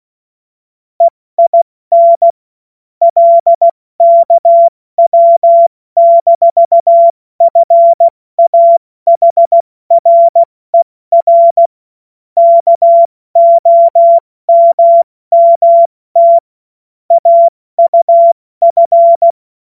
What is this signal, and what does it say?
Morse code, heard as a keyed tone, about 16 wpm, 680 Hz. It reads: EIN LKW-FAHRER KOMMT AUF
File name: Morsekurs / Witze / Witz 20. September 2022 70 F80